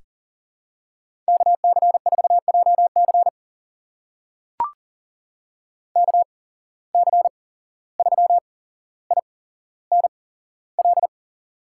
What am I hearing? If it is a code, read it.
Morse code, heard as a keyed tone, about 39 wpm, spaced as for 16 wpm, 700 Hz. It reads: KC4JC KC3INL
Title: USA Call Signs - 8x Character Spacing - 40wpm